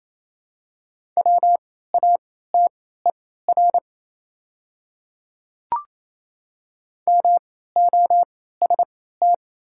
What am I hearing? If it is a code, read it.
Morse code, heard as a keyed tone, about 27 wpm, spaced as for 16 wpm, 690 Hz. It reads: WATER MOST
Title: 3x Character Spacing - Top 100 Words - 28wpm